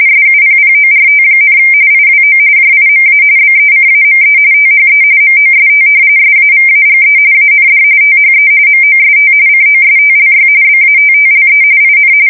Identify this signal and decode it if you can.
RTTY, about 45.45 baud. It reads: SUPPORT, IWO JIMA BECAME THE MOST SAVAGE AND THE MOST COSTLY BATTLE IN
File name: fsk170 / battleshipnj